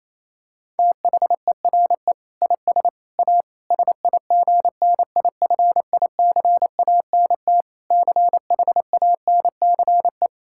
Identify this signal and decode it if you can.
Morse code, heard as a keyed tone, about 28 wpm, 700 Hz. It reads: THERE IS A SIGNIFICANT CHANCE
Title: Sentences from Top 1000 Words - Review Part 6 - 28wpm